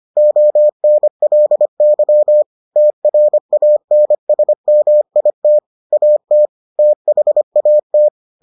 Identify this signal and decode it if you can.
Morse code, heard as a keyed tone, about 25 wpm, 600 Hz. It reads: ONLY TRANSMIT AT THAT